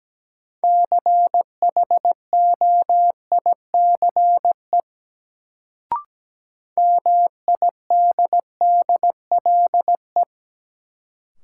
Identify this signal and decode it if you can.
Morse code, heard as a keyed tone, about 17 wpm, 700 Hz. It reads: CHOICE MIDDLE
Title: Sentences from Top 900 Words - New Word Drill - 17wpm